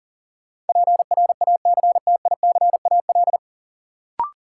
Morse code, heard as a keyed tone, about 40 wpm, 700 Hz. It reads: PRACTICAL
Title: Sentences from Top 1400 Words - New Word Drill - 40wpm